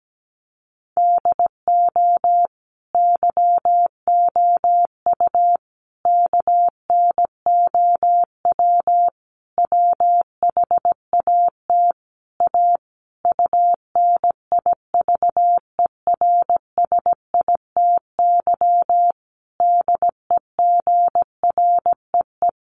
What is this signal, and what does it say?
Morse code, heard as a keyed tone, about 17 wpm, 700 Hz. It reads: DO YOU KNOW WHAT A UNIVERSITY DEGREE
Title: Sentences from Top 1500 Words 17wpm